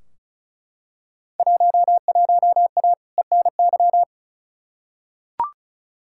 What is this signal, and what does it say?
Morse code, heard as a keyed tone, about 36 wpm, 700 Hz. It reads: 11A ENY
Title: Field Day Exchanges Encore - 35wpm